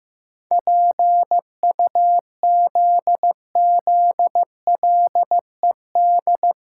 Morse code, heard as a keyed tone, about 14 wpm, 700 Hz. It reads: PUZZLED
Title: Pangram Challenge v2 - 15wpm